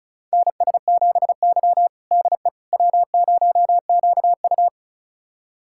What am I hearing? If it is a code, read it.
Morse code, heard as a keyed tone, about 36 wpm, 700 Hz. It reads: NS7Y DE W0QU
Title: USA Callsign DE Callsign - 4 Characters - 35wpm